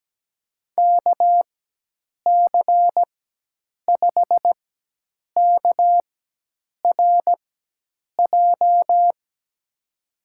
Morse code, heard as a keyed tone, about 17 wpm, spaced as for 8 wpm, 700 Hz. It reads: KC5KRJ
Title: USA Call Signs - 6 Characters - 4x Character Spacing - 17wpm